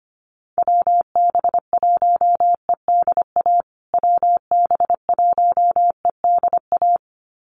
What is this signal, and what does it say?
Morse code, heard as a keyed tone, about 25 wpm, 700 Hz. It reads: WB1EDA WB1EDA
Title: USA Call Signs Repeated 2x 25wpm